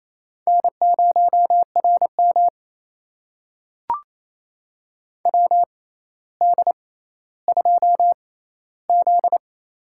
Morse code, heard as a keyed tone, about 28 wpm, spaced as for 14 wpm, 700 Hz. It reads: N0RM WD2Z
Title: USA Call Signs - 4 Characters - 6x Character Spacing - 28wpm